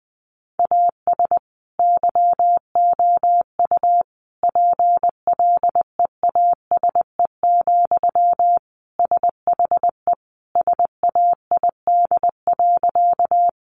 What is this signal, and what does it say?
Morse code, heard as a keyed tone, about 20 wpm, 700 Hz. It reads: AS YOU PLEASE, SHE SAID.
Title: Sentences from Top 1500 Words 20wpm